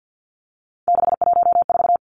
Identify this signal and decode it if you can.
Morse code, heard as a keyed tone, about 50 wpm, 700 Hz. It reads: 614
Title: Numbers - 3-Digit 50wpm